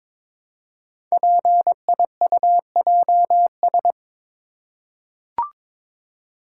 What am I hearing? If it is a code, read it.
Morse code, heard as a keyed tone, about 22 wpm, 700 Hz. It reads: PIUJS
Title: Code Groups - 5 Letters - 22wpm